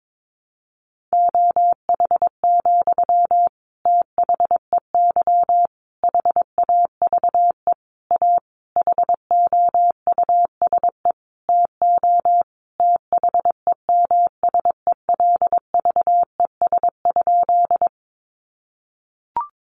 Morse code, heard as a keyed tone, about 22 wpm, 700 Hz. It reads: OH, THEY HAVE A HOUSE TO THEMSELVES?
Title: Sentences from Top 800 Words 22wpm